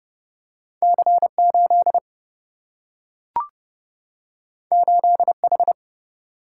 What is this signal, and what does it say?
Morse code, heard as a keyed tone, about 30 wpm, 700 Hz. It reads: C 8 8 H
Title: Single-Letter Single-Number 30wpm